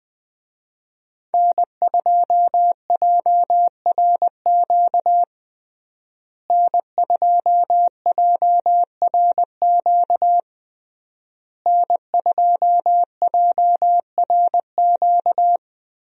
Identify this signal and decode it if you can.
Morse code, heard as a keyed tone, about 20 wpm, 700 Hz. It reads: N2JRQ N2JRQ N2JRQ
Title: USA Call Signs - 5 Characters - Repeated 3x - 20wpm